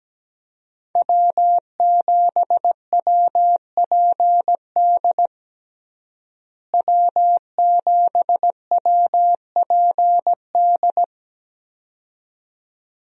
Morse code, heard as a keyed tone, about 17 wpm, 700 Hz. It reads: W7WPD W7WPD
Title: USA Call Signs - 5 Characters - Repeated 2x - 17wpm